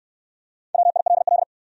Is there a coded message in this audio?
Yes: Morse code from 0.7 to 1.4 s, heard as a keyed tone, about 57 wpm, 700 Hz. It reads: WILL